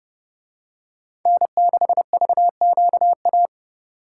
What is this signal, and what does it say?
Morse code, heard as a keyed tone, about 29 wpm, 700 Hz. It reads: N6VQA